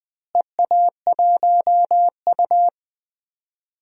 Morse code, heard as a keyed tone, about 20 wpm, 700 Hz. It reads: EA1U
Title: DXCC-100 Call Signs - 4 Characters - Repeated 2x - 20wpm